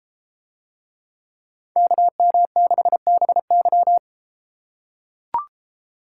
Morse code, heard as a keyed tone, about 33 wpm, 700 Hz. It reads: KM6BY